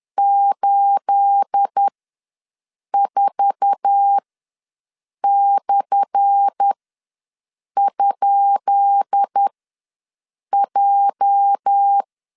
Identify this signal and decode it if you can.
Morse code, heard as a keyed tone, about 10 wpm, spaced as for 6 wpm, 800 Hz. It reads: 84/?J
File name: LEZIONI / 10 SIGNS / 10B INT ws 10wpm 4134